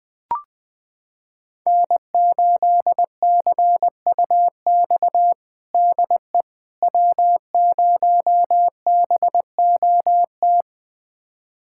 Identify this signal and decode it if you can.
Morse code, heard as a keyed tone, about 20 wpm, 700 Hz. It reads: N8CUX DE W0BOT
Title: USA Callsign DE Callsign - 5 Characters - 20wpm